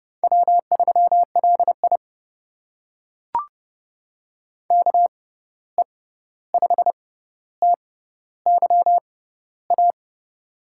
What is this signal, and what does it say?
Morse code, heard as a keyed tone, about 30 wpm, spaced as for 14 wpm, 700 Hz. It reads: W3LI KE5TYA
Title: USA Call Signs - 6x Character Spacing - 30wpm